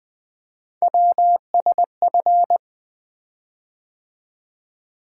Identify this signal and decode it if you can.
Morse code, heard as a keyed tone, about 20 wpm, 700 Hz. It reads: WSF
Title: Three Letter-Number Combinations - 20wpm